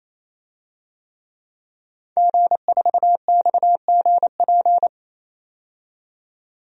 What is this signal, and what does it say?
Morse code, heard as a keyed tone, about 28 wpm, 700 Hz. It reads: G4XGP
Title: DXCC-100 Call Signs - 5 Characters - 28wpm